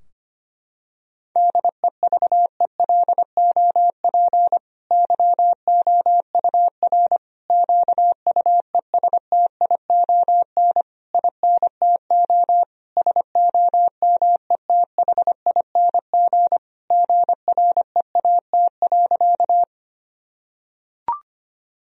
Morse code, heard as a keyed tone, about 25 wpm, 700 Hz. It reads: DEVELOP YOUR QUESTION INTO SOMETHING GREAT.
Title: Sentences from Top 1000 Words - Review Part 2 - 25wpm